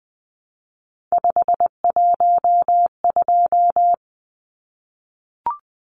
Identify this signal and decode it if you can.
Morse code, heard as a keyed tone, about 20 wpm, 700 Hz. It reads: 512
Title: Numbers - 3-Digit 20wpm